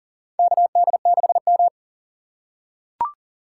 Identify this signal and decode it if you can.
Morse code, heard as a keyed tone, about 40 wpm, 700 Hz. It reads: KD6M
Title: USA Call Signs - 4 Characters - 3x Character Spacing - 40wpm